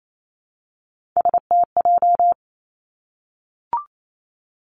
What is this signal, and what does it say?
Morse code, heard as a keyed tone, about 28 wpm, 700 Hz. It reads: STJ
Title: Three Challenging Letter-Number Combinations 28wpm